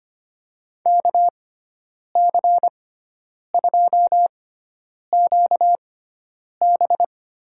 Morse code, heard as a keyed tone, about 25 wpm, spaced as for 9 wpm, 700 Hz. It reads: KC2QB
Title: USA Call Signs - 6 Characters - 6x Character Spacing - 25wpm